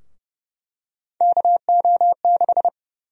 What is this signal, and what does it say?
Morse code, heard as a keyed tone, about 30 wpm, 700 Hz. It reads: KO6
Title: USA Call Sign Prefixes - Repeated 2x - 30wpm